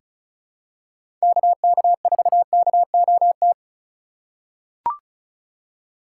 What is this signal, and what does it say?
Morse code, heard as a keyed tone, about 34 wpm, 700 Hz. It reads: KK4KOT
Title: USA Call Signs - 6 Characters - 8x Character Spacing - 35wpm